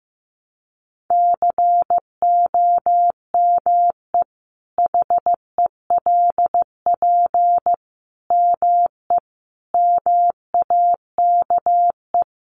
Morse code, heard as a keyed tone, about 15 wpm, 700 Hz. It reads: COME HELP ME MAKE
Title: Sentences from Top 1300 Words 15wpm